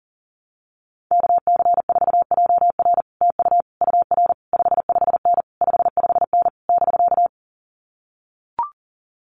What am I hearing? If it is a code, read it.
Morse code, heard as a keyed tone, about 42 wpm, 700 Hz. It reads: KC4JR TU UR 55N 55N <BK>
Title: Common QSO Phrases Encore 40wpm